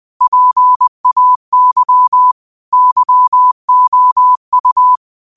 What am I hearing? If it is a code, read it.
Morse code, heard as a keyed tone, about 20 wpm, 1000 Hz. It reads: PAY YOU